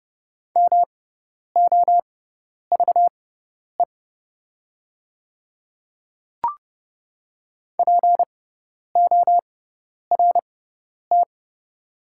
Morse code, heard as a keyed tone, about 30 wpm, spaced as for 10 wpm, 690 Hz. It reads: MOVE PORT
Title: 6x Character Spacing - Top 200 Words - 30wpm